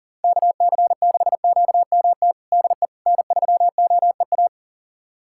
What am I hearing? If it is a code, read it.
Morse code, heard as a keyed tone, about 40 wpm, 700 Hz. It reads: KC6QMT DE N3OEA
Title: USA Callsign DE Callsign 40wpm